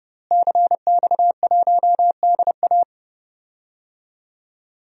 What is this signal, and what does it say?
Morse code, heard as a keyed tone, about 29 wpm, 700 Hz. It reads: CX1DA
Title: DXCC-100 Call Signs - 5 Characters - Repeated 3x - 30wpm